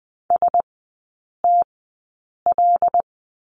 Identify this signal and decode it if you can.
Morse code, heard as a keyed tone, about 20 wpm, spaced as for 8 wpm, 700 Hz. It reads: STL
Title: 093 Lesson 16e - Letter c - Three Characters 20wpm